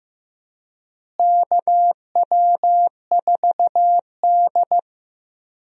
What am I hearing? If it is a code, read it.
Morse code, heard as a keyed tone, about 15 wpm, 700 Hz. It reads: KW4D
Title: USA Call Signs - 4 Characters - Repeated 3x - 15wpm